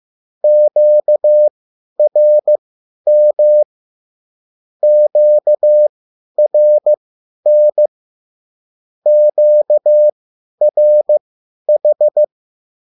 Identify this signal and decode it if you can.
Morse code, heard as a keyed tone, about 15 wpm, spaced as for 10 wpm, 600 Hz. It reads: QRM QRN QRH